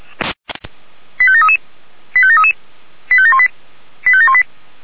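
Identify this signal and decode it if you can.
Five-tone selective call (ZVEI), heard as quick runs of five tones, tones of about 70 ms. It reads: 85733 85733 85718 85718
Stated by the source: ZVEI five-tone selective call received on SDR.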